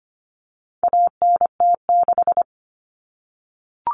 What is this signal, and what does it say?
Morse code, heard as a keyed tone, about 25 wpm, 700 Hz. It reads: ANT6
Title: Common Cut Numbers - 4-Digit 25wpm